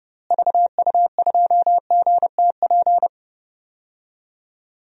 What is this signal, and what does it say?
Morse code, heard as a keyed tone, about 30 wpm, 700 Hz. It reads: VU2GTP
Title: DXCC-100 Call Signs - 6 Characters - Repeated 3x - 30wpm